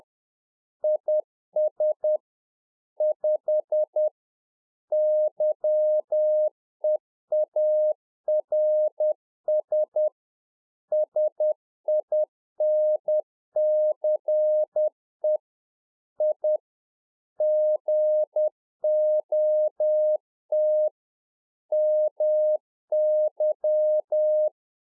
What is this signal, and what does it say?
Morse code, heard as a keyed tone, about 10 wpm, 610 Hz. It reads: IS 5 YEARS SINCE I GOT MY
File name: 2011-04-mp3 / RICK JJ0MPI 10WPM